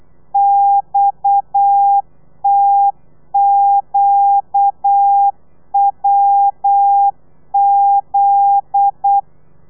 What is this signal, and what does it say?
Morse code, heard as a keyed tone, about 8 wpm, 800 Hz. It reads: XTQWZ
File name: telegrafia / i2tfj / Lez9 35